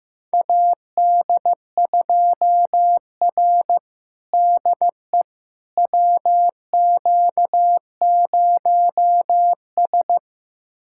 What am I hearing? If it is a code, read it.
Morse code, heard as a keyed tone, about 16 wpm, 700 Hz. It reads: AD2R DE WQ0S
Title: USA Callsign DE Callsign - 4 Characters - 15wpm